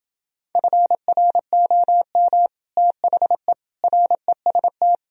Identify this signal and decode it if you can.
Morse code, heard as a keyed tone, about 29 wpm, 700 Hz. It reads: FROM THE REST